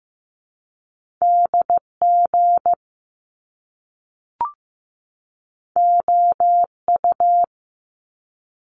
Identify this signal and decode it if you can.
Morse code, heard as a keyed tone, about 15 wpm, 700 Hz. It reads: DG OU
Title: Top Two Letter Combinations in Words 15wpm